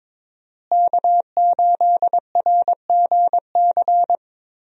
Morse code, heard as a keyed tone, about 22 wpm, 700 Hz. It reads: K8RGC
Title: USA Call Signs - 5 Characters - Repeated 2x - 22wpm